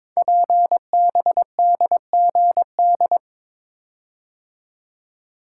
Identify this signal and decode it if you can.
Morse code, heard as a keyed tone, about 22 wpm, 700 Hz. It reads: PBDGD
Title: Code Groups - 5 Letters - 22wpm